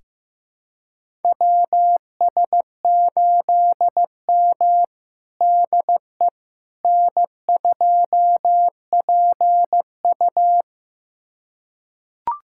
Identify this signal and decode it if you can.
Morse code, heard as a keyed tone, about 15 wpm, 700 Hz. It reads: WS8M DE N2PU
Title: USA Callsign DE Callsign - 4 Characters - 15wpm